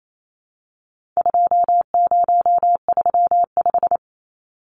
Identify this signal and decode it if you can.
Morse code, heard as a keyed tone, about 28 wpm, 700 Hz. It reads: 2035
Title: Numbers - 4-Digit 28wpm